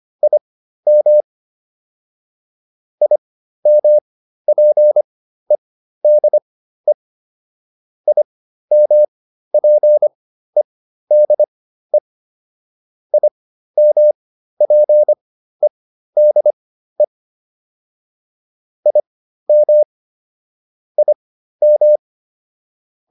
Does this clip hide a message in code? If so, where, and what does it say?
Yes: Morse code from 0.2 to 22.0 s, heard as a keyed tone, about 25 wpm, spaced as for 13 wpm, 600 Hz. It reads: IM IMPEDE IMPEDE IMPEDE IM IM